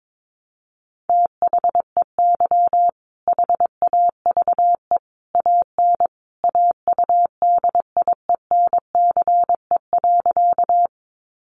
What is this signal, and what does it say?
Morse code, heard as a keyed tone, about 22 wpm, 700 Hz. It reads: THEY HAVE AN AUDIENCE.
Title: Sentences from Top 800 Words 22wpm